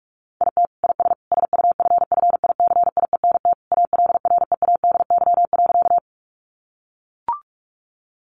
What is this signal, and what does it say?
Morse code, heard as a keyed tone, about 44 wpm, 700 Hz. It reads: IT IS SUFFICIENT ALREADY.
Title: Sentences from Top 1400 Words 45wpm